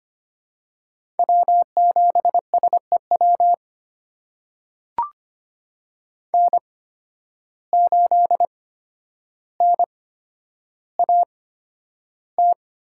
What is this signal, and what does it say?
Morse code, heard as a keyed tone, about 25 wpm, spaced as for 11 wpm, 700 Hz. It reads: W7SEW N8NAT
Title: USA Call Signs - 5 Characters - 8x Character Spacing - 25wpm